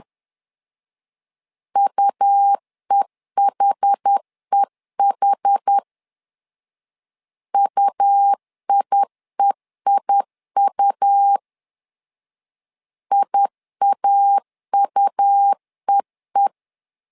Morse code, heard as a keyed tone, about 10 wpm, 800 Hz. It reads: UEHEH UIEIU IAUEE
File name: LEZIONI / 2 UHDN / 2 UH 10wpm 1714